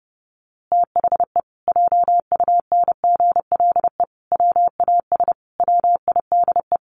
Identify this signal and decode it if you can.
Morse code, heard as a keyed tone, about 30 wpm, 700 Hz. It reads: THE JUNGLE WAS WIDE
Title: A Sound Of Thunder 30wpm